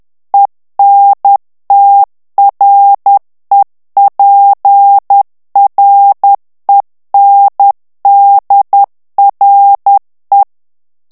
Morse code, heard as a keyed tone, about 10 wpm, 800 Hz. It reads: ENTREPRENDRE
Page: Français - texte en clair 1 - vitesse : 10 mots minute